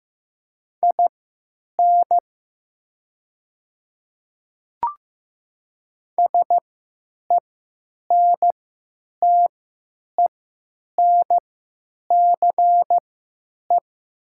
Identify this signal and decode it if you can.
Morse code, heard as a keyed tone, about 15 wpm, spaced as for 8 wpm, 700 Hz. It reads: IN SENTENCE
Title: 3x Character Spacing - Top 200 Words - 15wpm